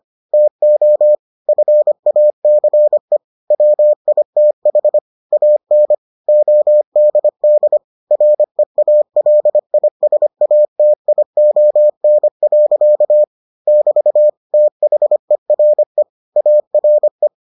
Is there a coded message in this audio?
Yes: Morse code from 0.3 to 17.3 s, heard as a keyed tone, about 25 wpm, 600 Hz. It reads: TO FACE WITH AN ODD REALISATION. <BT> THERE ARE